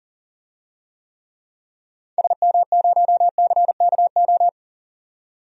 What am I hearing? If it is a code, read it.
Morse code, heard as a keyed tone, about 40 wpm, 700 Hz. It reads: SM0CKO
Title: DXCC-100 Call Signs - 6 Characters - 40wpm